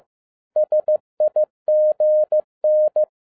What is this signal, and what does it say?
Morse code, heard as a keyed tone, about 15 wpm, 610 Hz. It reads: SIGN
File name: 2011-10-mp3 / n07-A DOT 15wpm.mp3